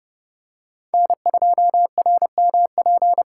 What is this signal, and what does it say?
Morse code, heard as a keyed tone, about 30 wpm, 700 Hz. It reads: N2RMP
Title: USA Call Signs - 5 Characters - 3x Character Spacing - 30wpm